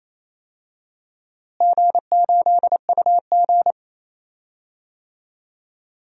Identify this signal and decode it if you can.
Morse code, heard as a keyed tone, about 28 wpm, 700 Hz. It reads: G8UG
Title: DXCC-100 Call Signs - 4 Characters - Repeated 2x - 28wpm